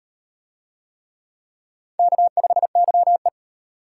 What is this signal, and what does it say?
Morse code, heard as a keyed tone, about 37 wpm, 700 Hz. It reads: K5YE